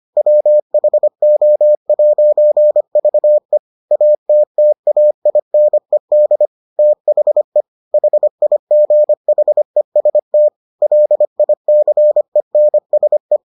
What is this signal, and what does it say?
Morse code, heard as a keyed tone, about 25 wpm, 600 Hz. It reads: WHO'VE ATTAINED THE HIGHEST LICENSE